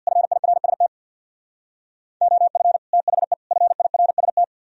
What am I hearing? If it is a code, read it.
Morse code, heard as a keyed tone, about 63 wpm, 700 Hz. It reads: FIRST OF THE FIRST